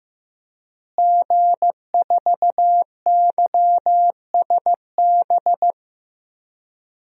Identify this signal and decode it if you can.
Morse code, heard as a keyed tone, about 15 wpm, 700 Hz. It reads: G4YSB
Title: DXCC-100 Call Signs - 5 Characters - Repeated 2x - 15wpm